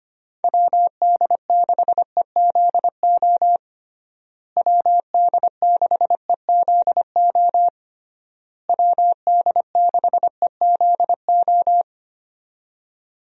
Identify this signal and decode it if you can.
Morse code, heard as a keyed tone, about 25 wpm, 700 Hz. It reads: WD6EZO WD6EZO WD6EZO
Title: USA Call Signs - 6 Characters - Repeated 3x - 25wpm